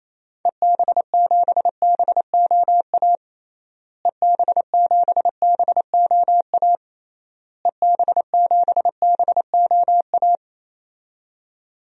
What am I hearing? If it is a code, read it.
Morse code, heard as a keyed tone, about 28 wpm, 700 Hz. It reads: EB7BOA EB7BOA EB7BOA
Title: DXCC-100 Call Signs - 6 Characters - Repeated 3x - 28wpm